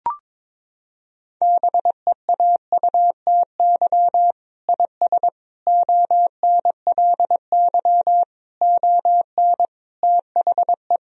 Morse code, heard as a keyed tone, about 22 wpm, 700 Hz. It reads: BEAUTY IS ONLY ON THE
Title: Sentences from Top 1000 Words - Review Part 3 - 22wpm